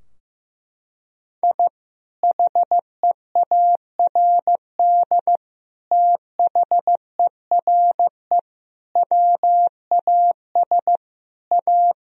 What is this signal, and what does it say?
Morse code, heard as a keyed tone, about 15 wpm, 700 Hz. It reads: I HEARD THERE WAS A
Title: Sentences from Top 1000 Words - Review Part 2 - 15wpm